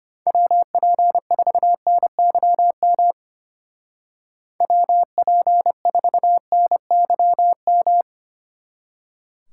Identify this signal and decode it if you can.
Morse code, heard as a keyed tone, about 32 wpm, 700 Hz. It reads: WP4NYM WP4NYM